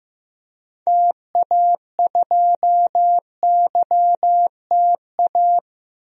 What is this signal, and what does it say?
Morse code, heard as a keyed tone, about 15 wpm, 700 Hz. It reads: TA2YTA
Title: DXCC-100 Call Signs - 6 Characters - 15wpm